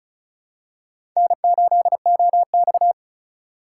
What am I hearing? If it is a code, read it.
Morse code, heard as a keyed tone, about 36 wpm, 700 Hz. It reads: N8OX
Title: USA Call Signs - 4 Characters - 2x Character Spacing - 35wpm